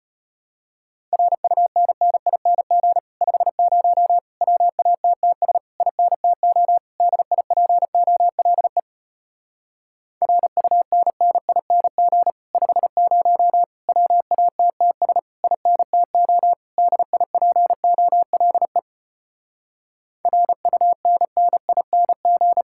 Morse code, heard as a keyed tone, about 33 wpm, 700 Hz. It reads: RUNNING 50 WATTS INTO DIPOLE RUNNING 50 WATTS INTO DIPOLE RUNNING